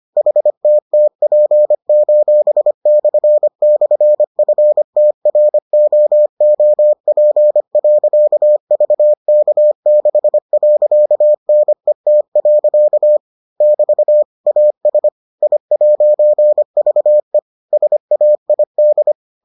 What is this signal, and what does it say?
Morse code, heard as a keyed tone, about 25 wpm, 600 Hz. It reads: HTTP://FTROOP.VK6.NET. <BT> AS I'VE SAID